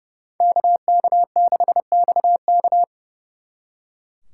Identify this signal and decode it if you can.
Morse code, heard as a keyed tone, about 39 wpm, 700 Hz. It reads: KK6XK